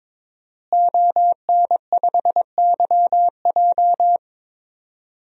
Morse code, heard as a keyed tone, about 22 wpm, 700 Hz. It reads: ON5YJ
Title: DXCC-100 Call Signs - 5 Characters - Repeated 3x - 22wpm